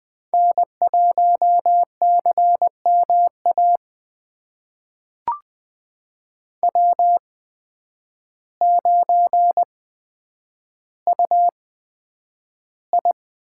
Morse code, heard as a keyed tone, about 20 wpm, spaced as for 9 wpm, 700 Hz. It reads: N1CMA W9UI
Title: USA Call Signs - 8x Character Spacing - 20wpm